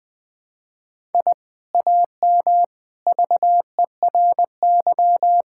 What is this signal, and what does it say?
Morse code, heard as a keyed tone, about 20 wpm, 700 Hz. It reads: I AM VERY
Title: Sentences from Top 1000 Words - Review Part 6 - 20wpm